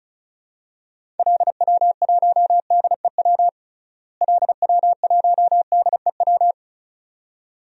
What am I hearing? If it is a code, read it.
Morse code, heard as a keyed tone, about 36 wpm, 700 Hz. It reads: LW1DEW LW1DEW
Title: DXCC-100 Call Signs - 6 Characters - Repeated 2x - 35wpm